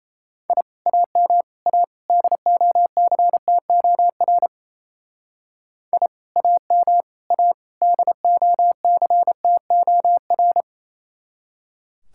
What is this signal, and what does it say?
Morse code, heard as a keyed tone, about 34 wpm, 700 Hz. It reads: I AM A DOCTOR I AM A DOCTOR
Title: Speed-Racing Common QSO Phrases - 28wpm